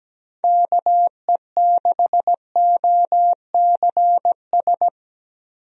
Morse code, heard as a keyed tone, about 17 wpm, 700 Hz. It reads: KE6OCS
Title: USA Call Signs - 6 Characters - Repeated 3x - 17wpm